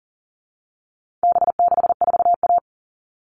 Morse code, heard as a keyed tone, about 40 wpm, 700 Hz. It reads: B64A
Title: All Cut Numbers - 4-Digit 40wpm